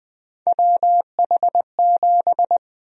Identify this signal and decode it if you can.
Morse code, heard as a keyed tone, about 21 wpm, 700 Hz. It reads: WH7
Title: USA Call Sign Prefixes - 20wpm